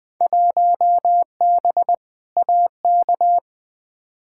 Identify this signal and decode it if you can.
Morse code, heard as a keyed tone, about 20 wpm, 700 Hz. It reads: 1B AK
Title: Field Day Exchanges Encore - 20wpm